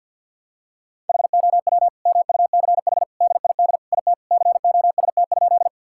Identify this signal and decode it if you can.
Morse code, heard as a keyed tone, about 50 wpm, 700 Hz. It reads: HOW MUCH DID IT COST?
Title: Sentences from Top 1000 Words - Review Part 5 - 50wpm